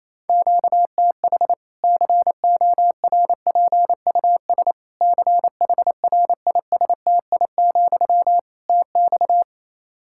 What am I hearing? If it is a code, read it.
Morse code, heard as a keyed tone, about 28 wpm, 700 Hz. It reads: QTH CORPUS CHRISTI, TX
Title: Practice QSO - Exchange 1 - Beginning - 28wpm